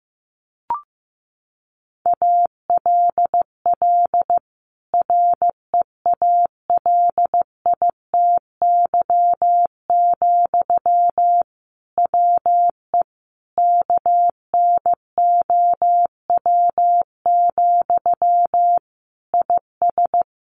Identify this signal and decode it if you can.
Morse code, heard as a keyed tone, about 14 wpm, 700 Hz. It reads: ALL REALITY, WE KNOW, IS
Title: Sentences from Top 1400 Words 15wpm